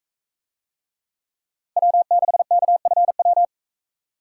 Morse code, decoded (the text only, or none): W6KFW